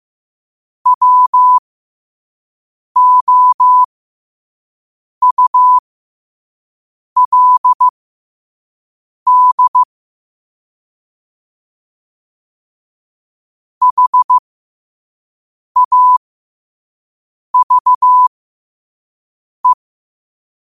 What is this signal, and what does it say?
Morse code, heard as a keyed tone, about 15 wpm, spaced as for 5 wpm, 1000 Hz. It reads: WOULD HAVE
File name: qotdfiles / 5wpm